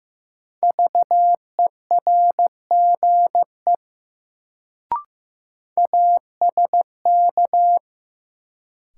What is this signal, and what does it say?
Morse code, heard as a keyed tone, about 15 wpm, 700 Hz. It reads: VERGE ASK
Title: Sets of 4 Words - New Word Drill - 15wpm